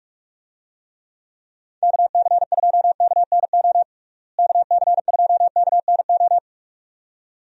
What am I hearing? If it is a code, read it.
Morse code, heard as a keyed tone, about 41 wpm, 700 Hz. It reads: KC2KNO KC2KNO
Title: USA Call Signs - 6 Characters - Repeated 2x - 45wpm